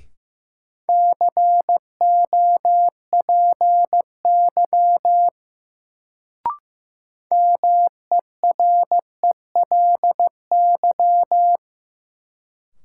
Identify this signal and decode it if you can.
Morse code, heard as a keyed tone, about 15 wpm, 700 Hz. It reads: COPY MERELY
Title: Sentences from Top 1400 Words - New Word Drill - 15wpm